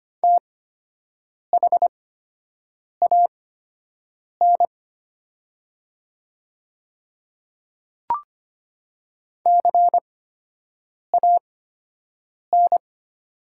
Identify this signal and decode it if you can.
Morse code, heard as a keyed tone, about 25 wpm, spaced as for 7 wpm, 700 Hz. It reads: THAN CAN
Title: 8x Character Spacing - Top 100 Words - 25wpm